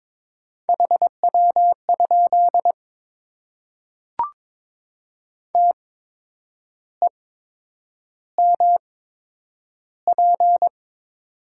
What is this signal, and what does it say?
Morse code, heard as a keyed tone, about 22 wpm, spaced as for 8 wpm, 700 Hz. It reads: HW? TEMP
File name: Top 90 QSO Elements - 8x Character Spacing - 22wpm